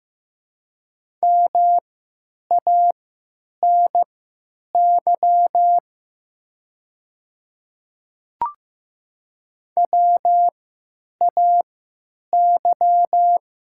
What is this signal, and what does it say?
Morse code, heard as a keyed tone, about 15 wpm, spaced as for 9 wpm, 700 Hz. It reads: MANY WAY
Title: 3x Character Spacing - Top 100 Words - 15wpm